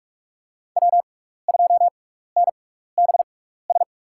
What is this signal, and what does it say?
Morse code, heard as a keyed tone, about 44 wpm, spaced as for 15 wpm, 700 Hz. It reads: W2NBS
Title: USA Call Signs - 6x Character Spacing - 45wpm